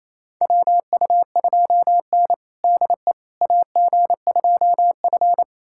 Morse code, heard as a keyed tone, about 28 wpm, 700 Hz. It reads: WU2N DE AG2F
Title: USA Callsign DE Callsign - 4 Characters - 28wpm